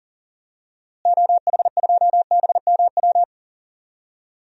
Morse code, heard as a keyed tone, about 40 wpm, 700 Hz. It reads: OH2BMW
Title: DXCC-100 Call Signs - 6 Characters - Repeated 2x - 40wpm